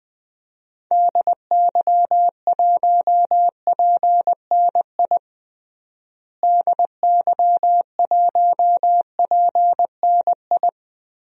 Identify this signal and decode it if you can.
Morse code, heard as a keyed tone, about 20 wpm, 700 Hz. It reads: DY1PNI DY1PNI
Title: DXCC-100 Call Signs - 6 Characters - Repeated 2x - 20wpm